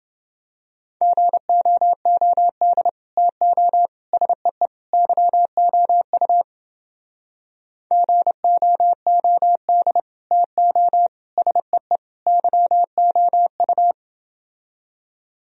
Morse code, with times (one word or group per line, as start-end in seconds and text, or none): 1.0-2.9 GOOD
3.2-3.9 TO
4.1-4.7 SEE
4.9-6.4 YOU
7.9-10.0 GOOD
10.3-11.1 TO
11.4-12.0 SEE
12.3-13.9 YOU